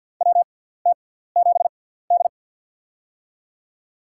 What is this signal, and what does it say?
Morse code, heard as a keyed tone, about 52 wpm, spaced as for 17 wpm, 700 Hz. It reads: WT7D
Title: USA Call Signs - 4 Characters - 6x Character Spacing - 50wpm